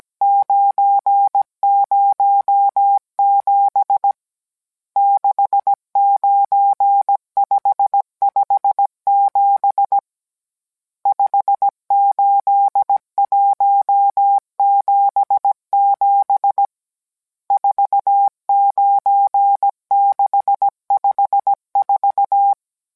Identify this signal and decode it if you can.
Morse code, heard as a keyed tone, about 17 wpm, 800 Hz. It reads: 907 69557 58177 49654